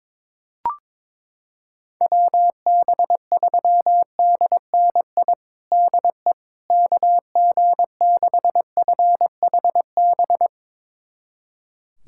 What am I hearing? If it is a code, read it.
Morse code, heard as a keyed tone, about 22 wpm, 700 Hz. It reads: WB3DNI DE KG6FHB
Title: USA Callsign DE Callsign - 6 Characters - 22wpm